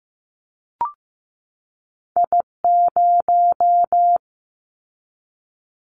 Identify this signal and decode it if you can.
Morse code, heard as a keyed tone, about 15 wpm, 700 Hz. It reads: I0
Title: Callsign Prefixes for Top 50 DXCC - 15wpm